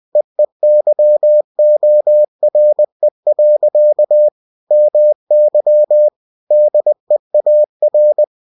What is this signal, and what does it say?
Morse code, heard as a keyed tone, about 20 wpm, 600 Hz. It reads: EEYORE. MY DEAR
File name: cw-books / winnie-the-pooh / winnie-20wpm-page-0032